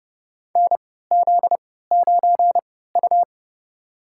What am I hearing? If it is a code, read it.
Morse code, heard as a keyed tone, about 30 wpm, spaced as for 17 wpm, 700 Hz. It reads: NZ9U
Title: USA Call Signs - 3x Character Spacing - 30wpm